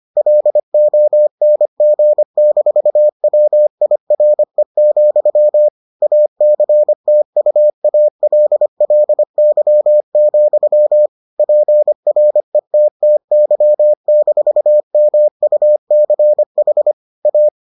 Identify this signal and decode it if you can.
Morse code, heard as a keyed tone, about 25 wpm, 600 Hz. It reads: LONG-WIRE, ACTUALLY, PRETTY-MUCH A